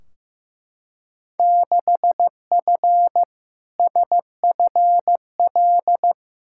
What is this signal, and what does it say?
Morse code, heard as a keyed tone, about 15 wpm, 700 Hz. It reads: 6F SFL
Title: Field Day Exchanges Encore - 15wpm